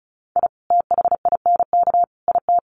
Morse code, heard as a keyed tone, about 34 wpm, 700 Hz. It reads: I THINK IT
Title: Sets of 3 Words 35wpm